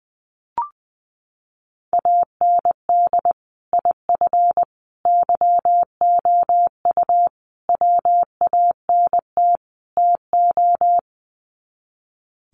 Morse code, heard as a keyed tone, about 20 wpm, 700 Hz. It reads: AND IF YOU WANT TO
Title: Sets of 5 Words Encore 20wpm